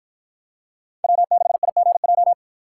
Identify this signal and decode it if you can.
Morse code, heard as a keyed tone, about 53 wpm, 700 Hz. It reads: W6IGJ